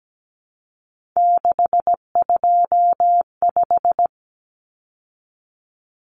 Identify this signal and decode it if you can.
Morse code, heard as a keyed tone, about 17 wpm, 700 Hz. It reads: 625
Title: Numbers - 3-Digit 17wpm